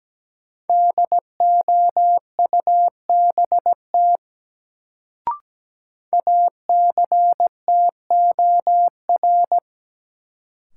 Morse code, heard as a keyed tone, about 17 wpm, 700 Hz. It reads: DOUBT ACTOR
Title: Sentences from Top 1500 Words - New Word Drill - 17wpm